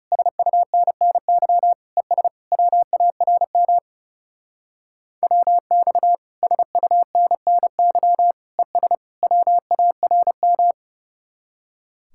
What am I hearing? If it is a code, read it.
Morse code, heard as a keyed tone, about 40 wpm, 700 Hz. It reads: SUNNY ES WARM WX SUNNY ES WARM